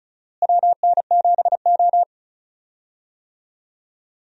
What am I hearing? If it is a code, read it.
Morse code, heard as a keyed tone, about 36 wpm, 700 Hz. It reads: WN7O
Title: USA Call Signs - 4 Characters - Repeated 3x - 35wpm